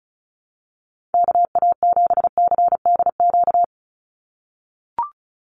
Morse code, heard as a keyed tone, about 34 wpm, 700 Hz. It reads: KA7CDQ
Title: USA Call Signs 35wpm